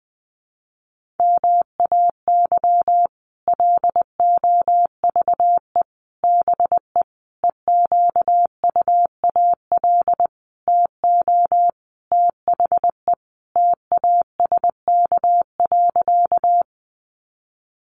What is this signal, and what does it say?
Morse code, heard as a keyed tone, about 20 wpm, 700 Hz. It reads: MAY LOVE BE EQUAL TO THE TASK.
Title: Sentences from Top 1400 Words 20wpm